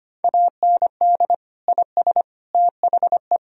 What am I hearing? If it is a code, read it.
Morse code, heard as a keyed tone, about 25 wpm, 700 Hz. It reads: AND IS THE
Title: Sentences from Top 1000 Words - Review Part 3 - 25wpm